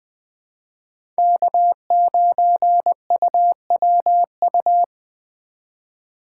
Morse code, heard as a keyed tone, about 20 wpm, 700 Hz. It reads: K9UWU
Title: USA Call Signs - 5 Characters - 2x Character Spacing - 20wpm